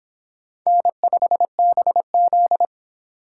Morse code, heard as a keyed tone, about 25 wpm, 700 Hz. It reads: N5BZ